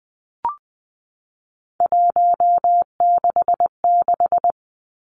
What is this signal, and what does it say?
Morse code, heard as a keyed tone, about 20 wpm, 700 Hz. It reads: 166
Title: Three Challenging Letter-Number Combinations 20wpm